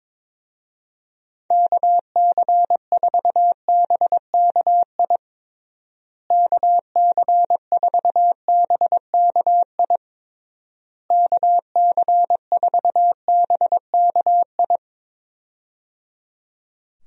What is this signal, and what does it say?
Morse code, heard as a keyed tone, about 22 wpm, 700 Hz. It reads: KC4BKI KC4BKI KC4BKI
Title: USA Call Signs - 6 Characters - Repeated 3x - 22wpm